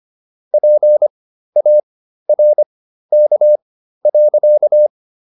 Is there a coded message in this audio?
Yes: Morse code from 0.5 to 4.9 s, heard as a keyed tone, about 25 wpm, spaced as for 13 wpm, 600 Hz. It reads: PARK.